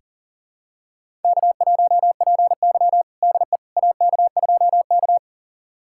Morse code, heard as a keyed tone, about 40 wpm, 700 Hz. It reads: K1PY DE AK2K
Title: USA Callsign DE Callsign - 4 Characters - 40wpm